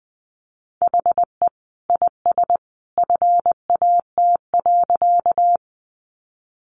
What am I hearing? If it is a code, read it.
Morse code, heard as a keyed tone, about 20 wpm, 700 Hz. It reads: HE IS FAT.
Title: Sentences from Top 600 Words 20wpm